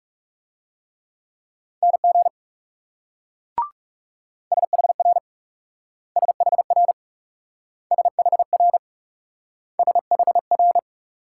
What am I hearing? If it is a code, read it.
Morse code, heard as a keyed tone, about 37 wpm, 700 Hz. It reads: NG SHR SHR SHR SHR
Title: Speed-Racing Multi-letter Phonemes - 30wpm